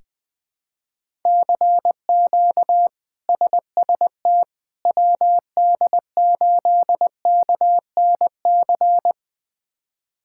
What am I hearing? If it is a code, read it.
Morse code, heard as a keyed tone, about 20 wpm, 700 Hz. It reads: CQ SST WD8KNC
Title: K1USN SST QSO Practice - 20wpm